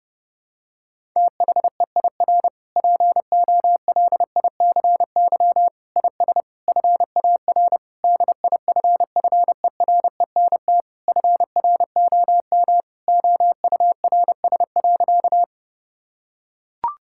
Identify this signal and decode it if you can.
Morse code, heard as a keyed tone, about 30 wpm, 700 Hz. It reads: THEIR POLICY IS FAR DIFFERENT FROM OURS.
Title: Sentences from Top 1000 Words - Review Part 5 - 30wpm